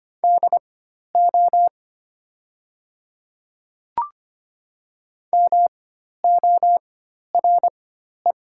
Morse code, heard as a keyed tone, about 25 wpm, spaced as for 12 wpm, 700 Hz. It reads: DO MORE
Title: 4x Character Spacing - Top 100 Words - 25wpm